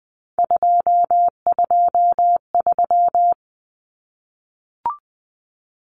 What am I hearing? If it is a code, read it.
Morse code, heard as a keyed tone, about 20 wpm, 700 Hz. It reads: 223
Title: Common Cut Numbers - 3-Digit 20wpm